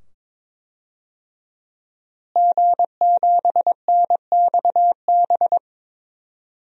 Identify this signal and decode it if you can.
Morse code, heard as a keyed tone, about 22 wpm, 700 Hz. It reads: G7NXB
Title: DXCC-100 Call Signs - 5 Characters - Repeated 2x - 22wpm